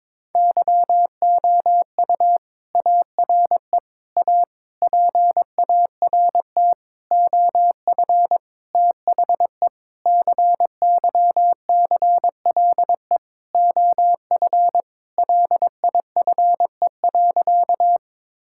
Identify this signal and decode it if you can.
Morse code, heard as a keyed tone, about 22 wpm, 700 Hz. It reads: YOU ARE A PART OF THE CYCLE OF LIFE.
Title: Sentences from Top 1000 Words - Review Part 6 - 22wpm